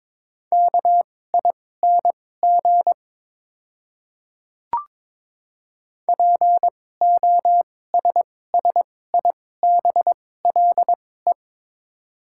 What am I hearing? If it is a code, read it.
Morse code, heard as a keyed tone, about 22 wpm, spaced as for 16 wpm, 700 Hz. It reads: KING POSSIBLE
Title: 2x Character Spacing - Top 500 Words - 22wpm